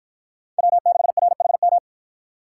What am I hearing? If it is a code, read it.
Morse code, heard as a keyed tone, about 53 wpm, 700 Hz. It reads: W6RHM